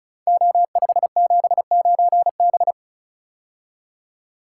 Morse code, heard as a keyed tone, about 36 wpm, 700 Hz. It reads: O579B
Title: Code Groups - 5 Letter:Numbers - 35wpm